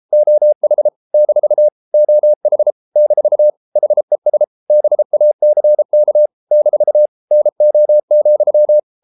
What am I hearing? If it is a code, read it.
Morse code, heard as a keyed tone, about 33 wpm, 600 Hz. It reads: OH - OH - HES BACK - NO,